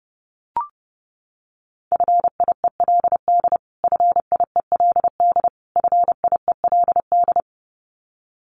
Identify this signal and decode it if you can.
Morse code, heard as a keyed tone, about 30 wpm, 700 Hz. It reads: FIELD FIELD FIELD
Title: Top 500 Words Repeated 30wpm